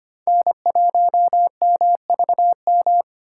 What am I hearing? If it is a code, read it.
Morse code, heard as a keyed tone, about 25 wpm, 700 Hz. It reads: N1MVM
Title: USA Call Signs - 5 Characters - 3x Character Spacing - 25wpm